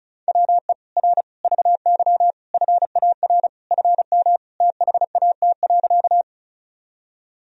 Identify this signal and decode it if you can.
Morse code, heard as a keyed tone, about 36 wpm, 700 Hz. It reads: WE R VY FAR FM THAT.
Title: Sentences from Top 300 Words and Abbreviations 35wpm